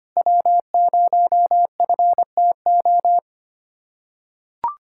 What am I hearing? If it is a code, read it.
Morse code, heard as a keyed tone, about 25 wpm, 700 Hz. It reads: W0FTO
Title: USA Callsign DE Callsign - 5 Characters - 25wpm